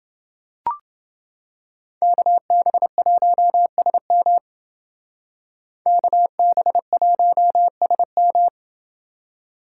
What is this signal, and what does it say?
Morse code, heard as a keyed tone, about 28 wpm, 700 Hz. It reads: KB1SM KB1SM